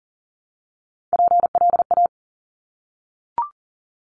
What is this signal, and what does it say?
Morse code, heard as a keyed tone, about 38 wpm, 700 Hz. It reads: PLA
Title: Top Three Letter Combinations in Words 40wpm